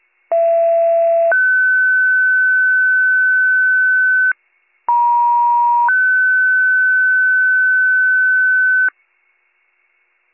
Two-tone page, heard as a one-second tone, then a three-second tone, about 670 then 1500 Hz, about 950 then 1500 Hz. It is heard from 0.3 to 8.9 s.